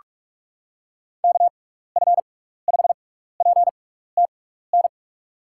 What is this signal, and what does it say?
Morse code, heard as a keyed tone, about 46 wpm, spaced as for 16 wpm, 700 Hz. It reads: KF5PTN
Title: USA Call Signs - 6x Character Spacing - 45wpm